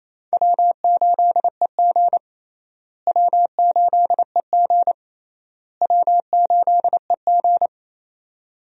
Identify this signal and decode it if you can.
Morse code, heard as a keyed tone, about 28 wpm, 700 Hz. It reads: W8EG W8EG W8EG
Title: USA Call Signs - 4 Characters - Repeated 3x - 28wpm